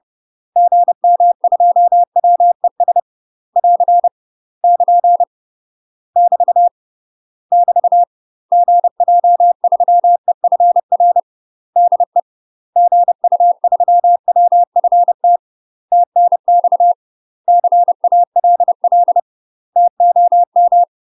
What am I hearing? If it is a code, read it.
Morse code, heard as a keyed tone, about 30 wpm, 700 Hz. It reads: GM2WES <AR> <KN> <BT> <BT> GJ3EFR DE GU3WFT TNX CALL TOM
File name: UK MASTER QSOs TRAINING / 30WPM / UK MASTER QSOs 0021